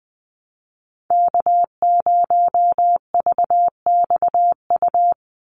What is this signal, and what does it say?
Morse code, heard as a keyed tone, about 20 wpm, 700 Hz. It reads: K0VXU
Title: Call Signs Repeated 3x - 20wpm